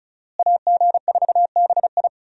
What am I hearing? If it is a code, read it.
Morse code, heard as a keyed tone, about 36 wpm, 700 Hz. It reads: AG4BI
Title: USA Call Signs - 5 Characters - Repeated 3x - 35wpm